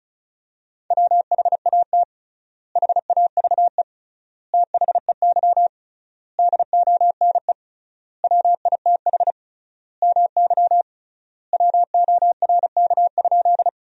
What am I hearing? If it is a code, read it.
Morse code, heard as a keyed tone, about 34 wpm, 700 Hz. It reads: WHAT HAVE THEY DONE WITH MY WORK?
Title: 3x Word-Spacing - Sentences from Top 500 Words - 35wpm